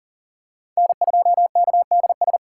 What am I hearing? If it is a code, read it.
Morse code, heard as a keyed tone, about 37 wpm, 700 Hz. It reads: N2KDS
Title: USA Callsign DE Callsign - 5 Characters - 40wpm